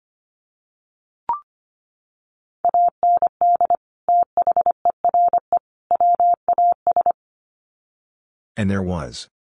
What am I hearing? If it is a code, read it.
Morse code, heard as a keyed tone, about 25 wpm, 700 Hz. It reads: AND THERE WAS
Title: Sets of 3 Words 25wpm